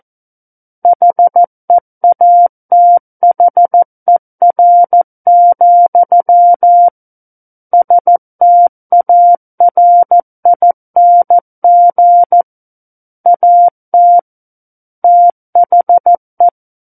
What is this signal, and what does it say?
Morse code, heard as a keyed tone, about 14 wpm, 700 Hz. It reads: HEATHER, STARING AT THE
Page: War of the Worlds - 05-Chapter 5 - 14 WPM